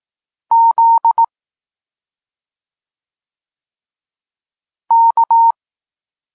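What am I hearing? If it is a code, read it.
Morse code, heard as a keyed tone, about 18 wpm, 920 Hz. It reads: Z K
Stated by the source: Polish Morse lesson teaching the letter C, from SP9MOA Niepołomicki Klub Krótkofalowców.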